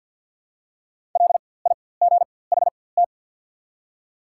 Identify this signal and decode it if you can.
Morse code, heard as a keyed tone, about 53 wpm, spaced as for 24 wpm, 700 Hz. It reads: LIGHT
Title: 4x Character Spacing - Top 300 Words - 50wpm